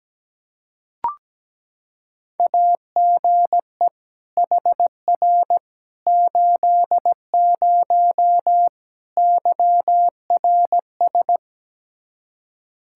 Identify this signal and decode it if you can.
Morse code, heard as a keyed tone, about 17 wpm, 700 Hz. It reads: AGE HR 80 YRS
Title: K1USN SST QSO Practice 17wpm